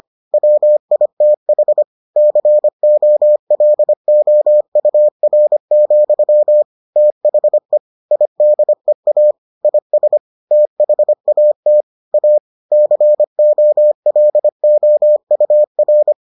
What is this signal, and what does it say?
Morse code, heard as a keyed tone, about 25 wpm, 600 Hz. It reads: WITH COLOUR, THE IDEA IS THAT A COLOUR